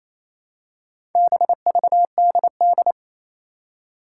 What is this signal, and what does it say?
Morse code, heard as a keyed tone, about 28 wpm, 700 Hz. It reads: BVDD
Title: Four Challenging Letter-Number Combinations 28wpm